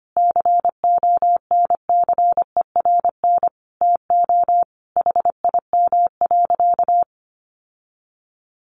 Morse code, heard as a keyed tone, about 25 wpm, 700 Hz. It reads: CONCERN TO HIM.
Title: Sentences from Top 1200 Words 25wpm